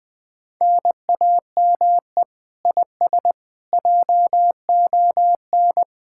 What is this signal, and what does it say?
Morse code, heard as a keyed tone, about 20 wpm, 700 Hz. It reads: NAME IS JON
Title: Speed-Racing Common QSO Phrases Encore - 17wpm